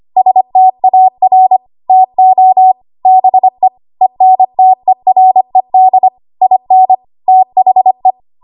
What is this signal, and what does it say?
Morse code, heard as a keyed tone, about 25 wpm, 750 Hz. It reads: STAR TO BE ENTERED IN THE
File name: Archive / 25 WPM / 25WPM